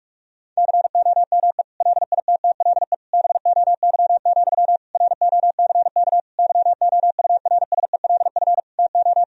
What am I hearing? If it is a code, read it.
Morse code, heard as a keyed tone, about 46 wpm, 700 Hz. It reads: COME LITTLE BOY, ROCK YOURSELF TO
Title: Sentences from Top 1000 Words - Review Part 2 - 45wpm